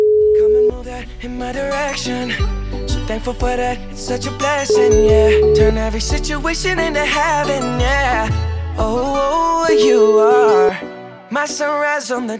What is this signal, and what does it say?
Ringback tone (Europe, 425 Hz), with speech.